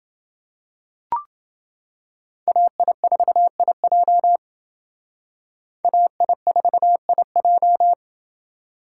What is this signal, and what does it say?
Morse code, heard as a keyed tone, about 28 wpm, 700 Hz. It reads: AI4IJ AI4IJ